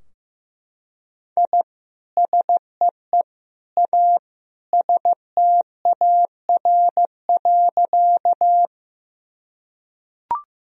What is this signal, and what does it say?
Morse code, heard as a keyed tone, about 15 wpm, 700 Hz. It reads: I SEE A STAR.
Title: Sentences from Top 1000 Words - Review Part 2 - 15wpm